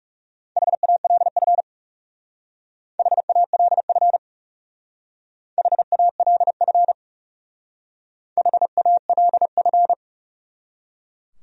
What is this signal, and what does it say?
Morse code, heard as a keyed tone, about 38 wpm, 700 Hz. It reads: HALF HALF HALF HALF